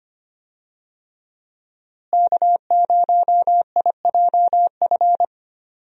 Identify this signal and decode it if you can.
Morse code, heard as a keyed tone, about 25 wpm, 700 Hz. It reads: K0IJF
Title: USA Call Signs - 5 Characters - Repeated 3x - 25wpm